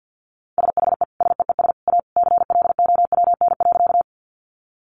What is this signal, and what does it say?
Morse code, heard as a keyed tone, about 50 wpm, 700 Hz. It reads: SHE SEES A CLOWN.
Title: Easy Phrases 50wpm